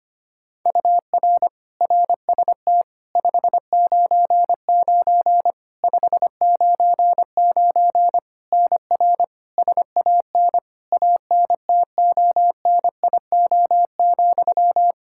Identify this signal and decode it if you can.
Morse code, heard as a keyed tone, about 25 wpm, 700 Hz. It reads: UR RST 599 599 NR SAN ANTONIO,
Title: K1USN SST QSO Practice 25wpm